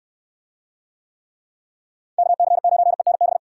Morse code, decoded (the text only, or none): KC9ENB